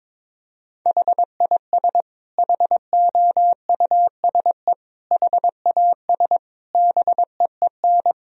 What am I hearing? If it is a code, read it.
Morse code, heard as a keyed tone, about 21 wpm, 700 Hz. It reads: HIS HOUSE HAS BEEN
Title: Sentences from Top 1000 Words - Review Part 3 - 22wpm